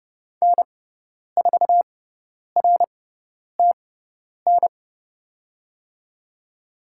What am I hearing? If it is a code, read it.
Morse code, heard as a keyed tone, about 31 wpm, spaced as for 10 wpm, 700 Hz. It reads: N4RTN
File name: Lesson 16g - Letter c - Call Signs 30wpm